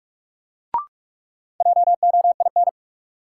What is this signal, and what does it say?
Morse code, heard as a keyed tone, about 41 wpm, 700 Hz. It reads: JOIN
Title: Sentences from Top 1000 Words - Review Part 6 - New Word Drill - 45wpm